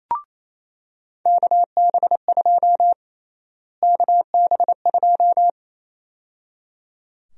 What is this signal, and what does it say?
Morse code, heard as a keyed tone, about 28 wpm, 700 Hz. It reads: KB2 KB2
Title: USA Call Sign Prefixes - Repeated 2x - 28wpm